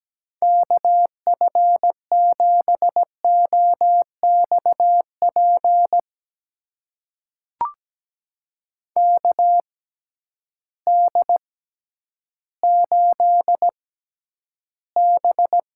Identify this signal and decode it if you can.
Morse code, heard as a keyed tone, about 17 wpm, spaced as for 10 wpm, 700 Hz. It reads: KF7OXP KD8B
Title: USA Call Signs - 6 Characters - 6x Character Spacing - 17wpm